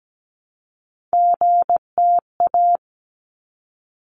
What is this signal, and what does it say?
Morse code, heard as a keyed tone, about 17 wpm, 700 Hz. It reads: GTA
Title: ARRL Sections 17wpm